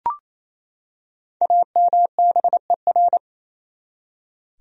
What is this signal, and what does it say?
Morse code, heard as a keyed tone, about 28 wpm, 700 Hz. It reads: AMBER
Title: USA Common Names - Max 5 Letters - 28wpm